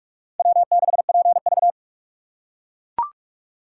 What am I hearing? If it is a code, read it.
Morse code, heard as a keyed tone, about 44 wpm, 700 Hz. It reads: W6PV